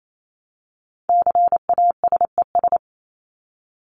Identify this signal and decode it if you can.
Morse code, heard as a keyed tone, about 28 wpm, 700 Hz. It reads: CASES
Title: ES:S Suffix - Top 100 Words with Base Words 28wpm